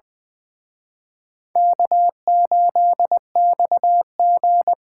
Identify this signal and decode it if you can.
Morse code, heard as a keyed tone, about 23 wpm, 700 Hz. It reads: K8XG
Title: Speed Racing - USA 1x2 Call Signs 20wpm